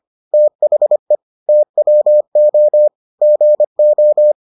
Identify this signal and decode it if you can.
Morse code, heard as a keyed tone, about 25 wpm, 600 Hz. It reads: THE TWO GO